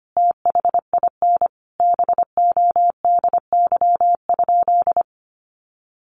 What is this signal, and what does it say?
Morse code, heard as a keyed tone, about 25 wpm, 700 Hz. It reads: THIN BODY?
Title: Sentences from Top 1200 Words 25wpm